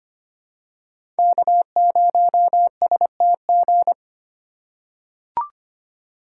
Morse code, heard as a keyed tone, about 25 wpm, 700 Hz. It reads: K0STG
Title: USA Call Signs - 5 Characters - 2x Character Spacing - 25wpm